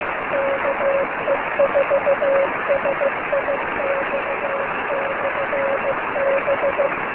Morse code, heard as a keyed tone, about 15 wpm, 550 Hz, with noise. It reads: KE4SIX/B
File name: beacons / ke4six-b